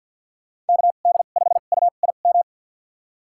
Morse code, heard as a keyed tone, about 52 wpm, spaced as for 37 wpm, 700 Hz. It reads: KD5UIM
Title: USA Call Signs - 2x Character Spacing - 50wpm